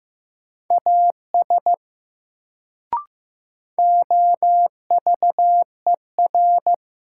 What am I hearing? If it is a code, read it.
Morse code, heard as a keyed tone, about 15 wpm, 700 Hz. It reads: AS OVER
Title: Sentences from Top 100 Words - New Word Drill - 15wpm